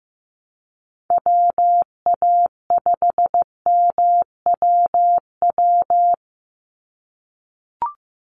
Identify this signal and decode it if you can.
Morse code, heard as a keyed tone, about 15 wpm, 700 Hz. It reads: WA5MWW
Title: USA Call Signs 15wpm